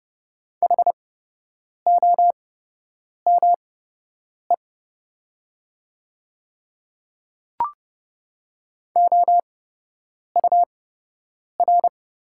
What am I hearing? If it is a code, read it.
Morse code, heard as a keyed tone, about 30 wpm, spaced as for 8 wpm, 700 Hz. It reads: HOME OUR
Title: 8x Character Spacing - Top 200 Words - 30wpm